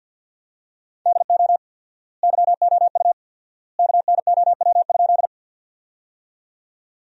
Morse code, heard as a keyed tone, about 53 wpm, 700 Hz. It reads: DO YOU KNOW?
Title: 4x Word-Spacing - Sentences from Top 100 Words - 50wpm